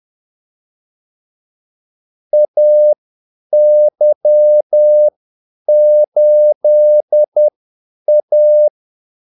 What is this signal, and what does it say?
Morse code, heard as a keyed tone, about 10 wpm, spaced as for 8 wpm, 600 Hz. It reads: AY8A
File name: Callsigns-10-8wpm